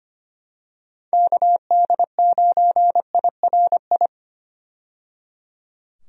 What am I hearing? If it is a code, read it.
Morse code, heard as a keyed tone, about 25 wpm, 700 Hz. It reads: KD9IRI
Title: USA Call Signs - 6 Characters - 25wpm